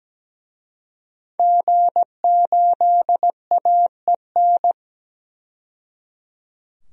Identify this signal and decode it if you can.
Morse code, heard as a keyed tone, about 17 wpm, 700 Hz. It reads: G8AEN
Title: DXCC-100 Call Signs - 5 Characters - 17wpm